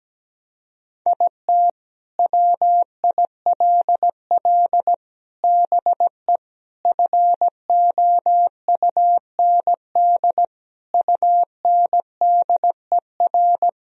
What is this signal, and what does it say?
Morse code, heard as a keyed tone, about 17 wpm, 700 Hz. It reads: IT WILL BE FOUND UNDER
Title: Sentences from Top 1000 Words - Review Part 1 - 17wpm